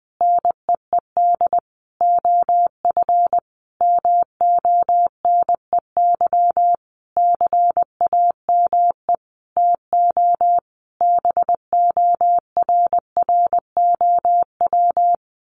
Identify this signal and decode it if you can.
Morse code, heard as a keyed tone, about 20 wpm, 700 Hz. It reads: NEED OF MONEY CAME TO BORROW
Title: Sentences from Top 1500 Words 20wpm